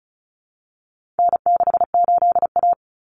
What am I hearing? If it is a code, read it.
Morse code, heard as a keyed tone, about 34 wpm, 700 Hz. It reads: N68A
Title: Common Cut Numbers - 4-Digit 35wpm